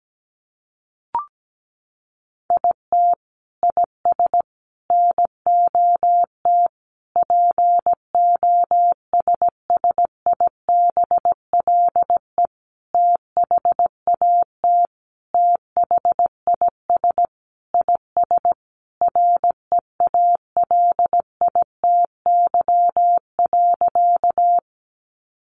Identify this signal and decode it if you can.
Morse code, heard as a keyed tone, about 17 wpm, 700 Hz. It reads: IT IS NOT POSSIBLE THAT THIS IS REALITY.
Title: Sentences from Top 1200 Words 17wpm